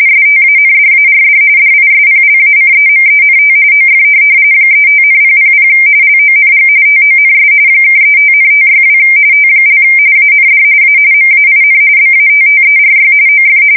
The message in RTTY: STANDARD DISPLACEMENT OF 35,000 LONG TONS (35,562 T) AND A MAIN BATTERY OF

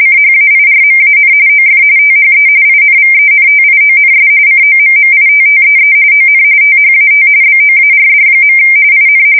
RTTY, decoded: CHAIRMAN OF THE INVENTIONS COMMITTEE ON THAT BOARD. IN

